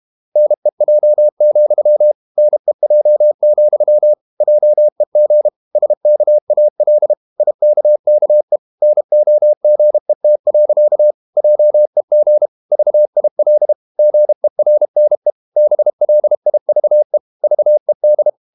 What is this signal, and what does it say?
Morse code, heard as a keyed tone, about 32 wpm, 600 Hz. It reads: NEJ, NEJ, JEG SKAL IKKE NOGET. JEG VIL GERNE BLIVE VED
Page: Ferie 32 wpm